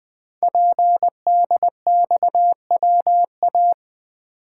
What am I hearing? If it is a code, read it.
Morse code, heard as a keyed tone, about 20 wpm, 700 Hz. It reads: PDXWA
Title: Code Groups - 5 Letter:Numbers - 20wpm